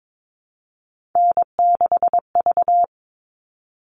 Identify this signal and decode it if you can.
Morse code, heard as a keyed tone, about 22 wpm, 700 Hz. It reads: N6V
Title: All Cut Numbers - 3-Digit 22wpm